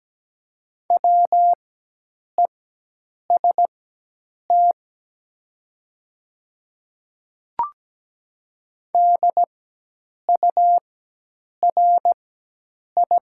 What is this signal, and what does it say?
Morse code, heard as a keyed tone, about 17 wpm, spaced as for 8 wpm, 700 Hz. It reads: WEST DURI
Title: 4x Character Spacing - Top 500 Words - 17wpm